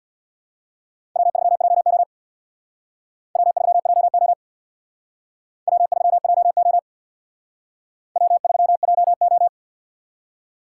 Morse code, heard as a keyed tone, about 62 wpm, 700 Hz. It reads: W3JO W3JO W3JO W3JO
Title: Speed Racing - USA 1x2 Call Signs 50wpm